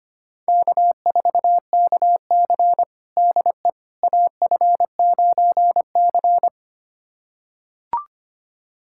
Morse code, heard as a keyed tone, about 25 wpm, 700 Hz. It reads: K4KC DE AF9C
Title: USA Callsign DE Callsign - 4 Characters - 25wpm